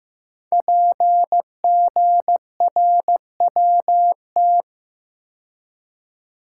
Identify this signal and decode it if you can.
Morse code, heard as a keyed tone, about 15 wpm, 700 Hz. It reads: PGRWT
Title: Code Groups - 5 Letters - 15wpm